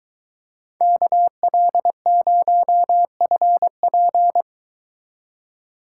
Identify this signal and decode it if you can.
Morse code, heard as a keyed tone, about 23 wpm, 700 Hz. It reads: KL0FP